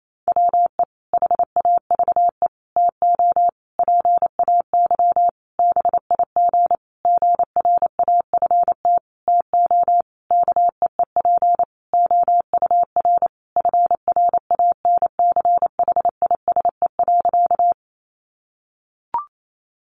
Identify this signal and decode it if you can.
Morse code, heard as a keyed tone, about 28 wpm, 700 Hz. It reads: WE HAVE TO PAY BIG GRAFT TO KEEP OUR FRANCHISE.
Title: A Sound Of Thunder 28wpm